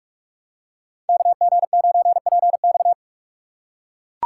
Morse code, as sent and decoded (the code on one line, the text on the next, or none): -.- --. ----. .--. -..-
KG9PX